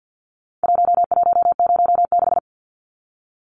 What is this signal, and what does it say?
Morse code, heard as a keyed tone, about 50 wpm, 700 Hz. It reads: 1106
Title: Numbers - Contest Style - 4-Digit 50wpm